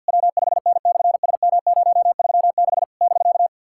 Morse code, heard as a keyed tone, about 50 wpm, 700 Hz. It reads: W5N/SM036 <BK>
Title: SOTA QSO Practice 50wpm